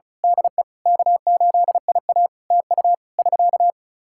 Morse code, heard as a keyed tone, about 34 wpm, 700 Hz. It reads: DE K8IA TU <SK>
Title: K1USN SST QSO Practice 35wpm